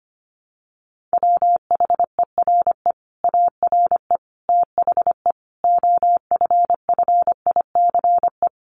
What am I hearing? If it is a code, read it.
Morse code, heard as a keyed tone, about 25 wpm, 700 Hz. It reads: WHERE ARE THE OFFICE
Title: Sentences from Top 1200 Words 25wpm